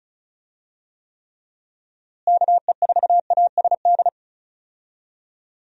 Morse code, heard as a keyed tone, about 34 wpm, 700 Hz. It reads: KE4ASD
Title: USA Call Signs - 6 Characters - Repeated 2x - 35wpm